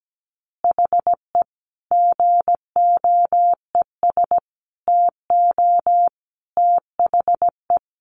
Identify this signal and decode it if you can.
Morse code, heard as a keyed tone, about 17 wpm, 700 Hz. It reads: HE GOES TO THE
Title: Additional Easy Phrases 17wpm